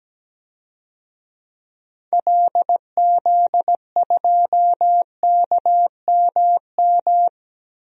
Morse code, heard as a keyed tone, about 17 wpm, 700 Hz. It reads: LZ2KMM
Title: DXCC-100 Call Signs - 6 Characters - Repeated 2x - 17wpm